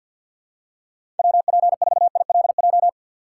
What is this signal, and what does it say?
Morse code, heard as a keyed tone, about 52 wpm, 700 Hz. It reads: WP4ILJ
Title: USA Call Signs - 4x Character Spacing - 50wpm